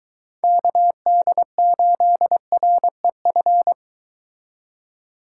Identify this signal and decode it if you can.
Morse code, heard as a keyed tone, about 23 wpm, 700 Hz. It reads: KD8REF